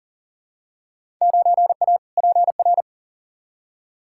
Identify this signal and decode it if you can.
Morse code, heard as a keyed tone, about 40 wpm, 700 Hz. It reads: 9A PR
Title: Field Day Exchanges Encore - 40wpm